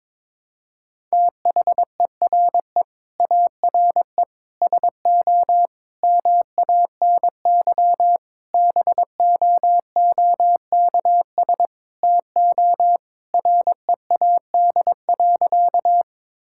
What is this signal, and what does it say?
Morse code, heard as a keyed tone, about 22 wpm, 700 Hz. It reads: THERE ARE SO MANY BOOKS TO READ.
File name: practice / 22wpm / Sentences with 4-Word Phrases - 22wpm